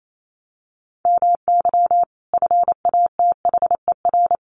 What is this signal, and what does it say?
Morse code, heard as a keyed tone, about 28 wpm, 700 Hz. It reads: MY FATHER
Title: Sentences from Top 1500 Words 28wpm